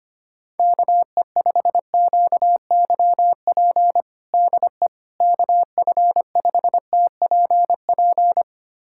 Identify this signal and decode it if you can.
Morse code, heard as a keyed tone, about 25 wpm, 700 Hz. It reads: KE5QYP DE KF5TPP
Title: USA Callsign DE Callsign - 6 Characters - 25wpm